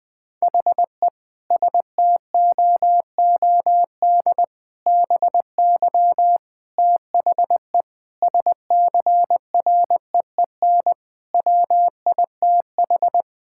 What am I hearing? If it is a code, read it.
Morse code, heard as a keyed tone, about 21 wpm, 700 Hz. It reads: HE STOOD BY THE SCREEN WITH
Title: Sentences from Top 1000 Words - Review Part 6 - 20wpm